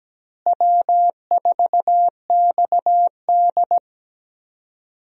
Morse code, heard as a keyed tone, about 17 wpm, 700 Hz. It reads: W4XD
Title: USA Call Signs - 4 Characters - Repeated 3x - 17wpm